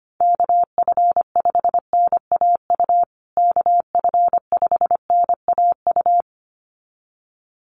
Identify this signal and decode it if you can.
Morse code, heard as a keyed tone, about 25 wpm, 700 Hz. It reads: KF5NAU KF5NAU
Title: USA Call Signs Repeated 2x 25wpm